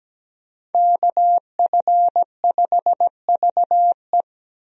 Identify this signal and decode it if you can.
Morse code, heard as a keyed tone, about 17 wpm, 700 Hz. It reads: KF5VE
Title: USA Call Signs - 5 Characters - Repeated 2x - 17wpm